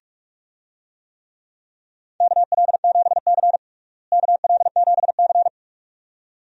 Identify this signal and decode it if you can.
Morse code, heard as a keyed tone, about 44 wpm, 700 Hz. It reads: KL7C KL7C
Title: DXCC-100 Call Signs - 4 Characters - Repeated 2x - 45wpm